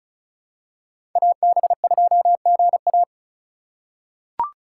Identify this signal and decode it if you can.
Morse code, heard as a keyed tone, about 34 wpm, 700 Hz. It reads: AB2GA
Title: USA Call Signs - 5 Characters - 2x Character Spacing - 35wpm